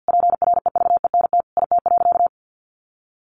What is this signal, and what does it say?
Morse code, heard as a keyed tone, about 50 wpm, 700 Hz. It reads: PREVENT IT.
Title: Sentences from Top 1100 Words 50wpm